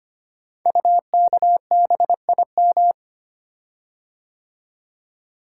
Morse code, heard as a keyed tone, about 25 wpm, 700 Hz. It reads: UKBIM
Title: Code Groups - 5 Letters - 25wpm